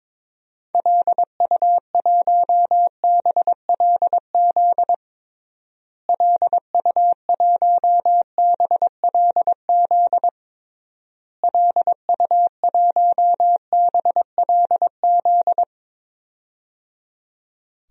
Morse code, heard as a keyed tone, about 22 wpm, 700 Hz. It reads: LU1BLZ LU1BLZ LU1BLZ
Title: DXCC-100 Call Signs - 6 Characters - Repeated 3x - 22wpm